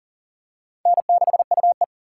Morse code, heard as a keyed tone, about 40 wpm, 700 Hz. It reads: N6UE
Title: USA Call Signs - 4x Character Spacing - 40wpm